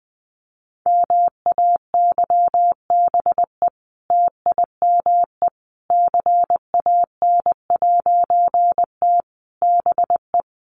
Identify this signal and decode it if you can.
Morse code, heard as a keyed tone, about 20 wpm, 700 Hz. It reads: MAYBE TIME CAN'T BE
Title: A Sound Of Thunder 20wpm